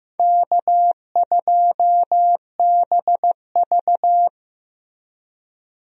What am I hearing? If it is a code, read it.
Morse code, heard as a keyed tone, about 15 wpm, 700 Hz. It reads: K2BV
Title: USA Call Signs - 4 Characters - Repeated 2x - 15wpm